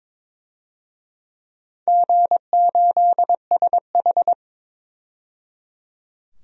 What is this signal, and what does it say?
Morse code, heard as a keyed tone, about 22 wpm, 700 Hz. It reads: G8SH
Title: DXCC-100 Call Signs - 4 Characters - 22wpm